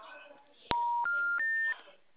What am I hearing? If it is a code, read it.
Special information tone, with speech.